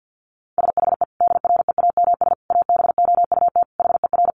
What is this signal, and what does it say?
Morse code, heard as a keyed tone, about 50 wpm, 700 Hz. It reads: SHE DREAMS ABOUT HER
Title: Easy Phrases 50wpm